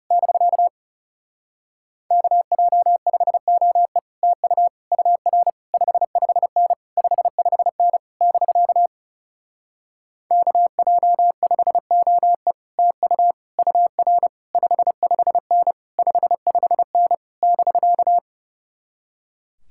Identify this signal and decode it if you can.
Morse code, heard as a keyed tone, about 35 wpm, 700 Hz. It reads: <BK> KJ5OE TU UR 55N 55N <BK> KJ5OE TU UR 55N 55N <BK>
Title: Speed-Racing Common QSO Phrases Encore - 30wpm